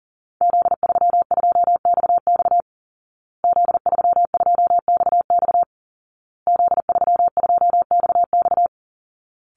Morse code, heard as a keyed tone, about 40 wpm, 700 Hz. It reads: Z32XX Z32XX Z32XX
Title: Call Signs Repeated 2x - 40wpm